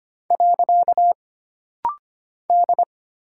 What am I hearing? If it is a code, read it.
Morse code, heard as a keyed tone, about 25 wpm, 700 Hz. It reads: . D
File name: Single Character (Letter:Number:Punctuation:Prosign) - Rapid-Fire - 25wpm